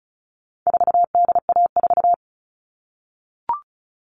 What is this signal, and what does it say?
Morse code, heard as a keyed tone, about 34 wpm, 700 Hz. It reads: 4DA4
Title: All Cut Numbers - 4-Digit 35wpm